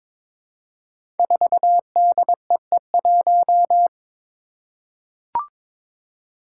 Morse code, heard as a keyed tone, about 22 wpm, 700 Hz. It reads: 4DEE1
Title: Five Challenging Letter-Number Combinations 22wpm